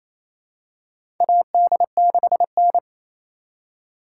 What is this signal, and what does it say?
Morse code, heard as a keyed tone, about 28 wpm, 700 Hz. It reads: AD6N
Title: USA Call Signs - 4 Characters - 2x Character Spacing - 28wpm